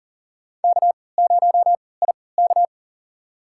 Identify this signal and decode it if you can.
Morse code, heard as a keyed tone, about 40 wpm, spaced as for 23 wpm, 700 Hz. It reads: K0IK
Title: USA Call Signs - 3x Character Spacing - 40wpm